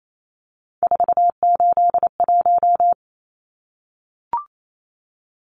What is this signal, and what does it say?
Morse code, heard as a keyed tone, about 28 wpm, 700 Hz. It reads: 481
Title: Numbers - 3-Digit 28wpm